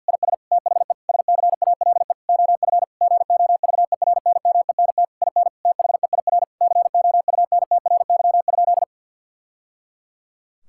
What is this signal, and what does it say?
Morse code, heard as a keyed tone, about 50 wpm, 700 Hz. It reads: IS THE SCALE OF GOVERNMENT IN THEIR COUNTRY?
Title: Sentences from Top 1000 Words - Review Part 5 - 50wpm